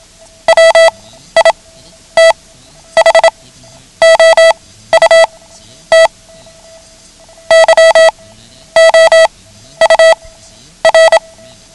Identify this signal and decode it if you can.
Morse code, heard as a keyed tone, about 27 wpm, spaced as for 11 wpm, 650 Hz. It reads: WITHOUT YOUR
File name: ham radio / CodeQuick / Code Quick 09